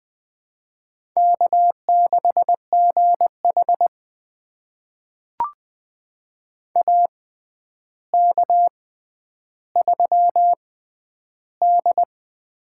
Morse code, heard as a keyed tone, about 20 wpm, spaced as for 10 wpm, 700 Hz. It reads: K6GH AK3D
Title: USA Call Signs - 4 Characters - 6x Character Spacing - 20wpm